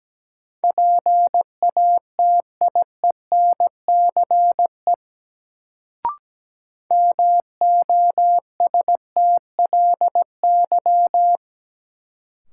Morse code, heard as a keyed tone, about 17 wpm, 700 Hz. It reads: PATIENCE MOSTLY
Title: Sentences from Top 1300 Words - New Word Drill - 17wpm